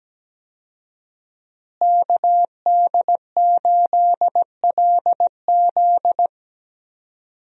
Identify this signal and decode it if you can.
Morse code, heard as a keyed tone, about 17 wpm, 700 Hz. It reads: KD8LZ
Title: USA Call Signs - 5 Characters - Repeated 3x - 17wpm